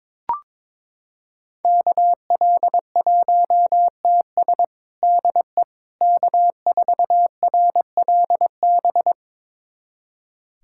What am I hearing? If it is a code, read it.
Morse code, heard as a keyed tone, about 22 wpm, 700 Hz. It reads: KL1TS DE K4RLB
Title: USA Callsign DE Callsign 22wpm